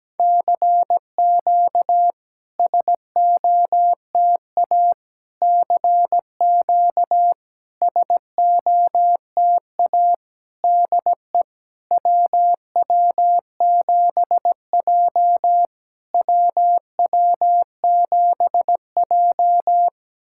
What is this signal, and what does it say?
Morse code, heard as a keyed tone, about 17 wpm, 700 Hz. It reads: CQ SOTA CQ SOTA DE WW7J WW7J
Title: SOTA QSO Practice 17wpm